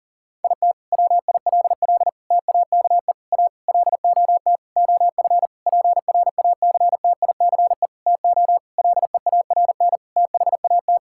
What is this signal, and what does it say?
Morse code, heard as a keyed tone, about 38 wpm, 700 Hz. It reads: IT WILL TAKE A LOT OF PRACTICE TO LEARN THAT
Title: Sentences from Top 1000 Words - Review Part 6 - 40wpm